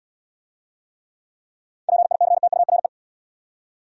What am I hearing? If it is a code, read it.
Morse code, heard as a keyed tone, about 72 wpm, 700 Hz. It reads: PICTURE